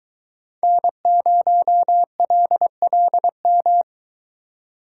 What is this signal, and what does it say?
Morse code, heard as a keyed tone, about 21 wpm, 700 Hz. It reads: N0LLM